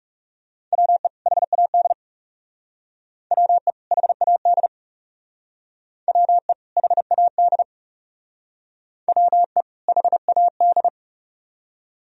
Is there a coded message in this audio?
Yes: Morse code from 0.7 to 10.9 s, heard as a keyed tone, about 38 wpm, 700 Hz. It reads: WE HAD WE HAD WE HAD WE HAD